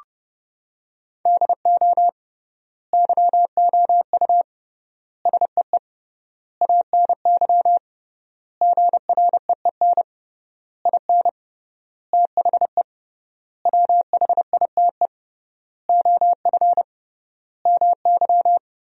Morse code, heard as a keyed tone, about 30 wpm, 700 Hz. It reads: DO YOU SEE ANY GREEN IN THE WHITE OF MY
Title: 3x Word-Spacing - Sentences from Top 500 Words - 30wpm